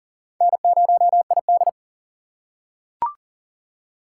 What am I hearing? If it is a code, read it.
Morse code, heard as a keyed tone, about 40 wpm, 700 Hz. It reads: N0ID
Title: USA Call Signs - 3x Character Spacing - 40wpm